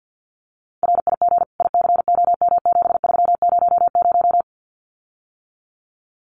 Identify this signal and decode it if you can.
Morse code, heard as a keyed tone, about 50 wpm, 700 Hz. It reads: RIG ICOM7300
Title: Common QSO Phrases Encore 50wpm